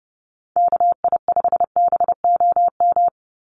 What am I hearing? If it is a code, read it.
Morse code, heard as a keyed tone, about 30 wpm, 700 Hz. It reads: KI5BOM
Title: USA Call Signs 30wpm